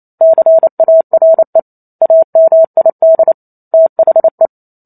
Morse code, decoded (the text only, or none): CARE AMID THE